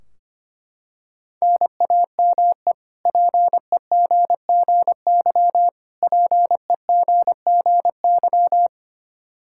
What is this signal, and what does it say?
Morse code, heard as a keyed tone, about 25 wpm, 700 Hz. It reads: NAME PEGGY PEGGY
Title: K1USN SST QSO Practice 25wpm